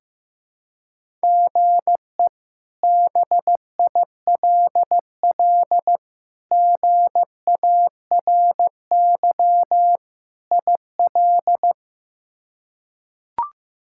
Morse code, heard as a keyed tone, about 15 wpm, 700 Hz. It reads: GE BILL GARY IL
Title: K1USN SST QSO Practice - 15wpm